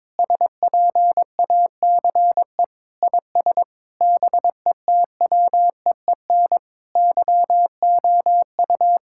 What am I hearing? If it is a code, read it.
Morse code, heard as a keyed tone, about 22 wpm, 700 Hz. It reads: SPACE IS BETWEEN YOU
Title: Sentences from Top 1000 Words - Review Part 6 - 22wpm